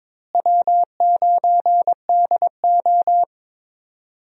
Morse code, heard as a keyed tone, about 22 wpm, 700 Hz. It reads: W9DO
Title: USA Call Signs - 4 Characters - Repeated 3x - 22wpm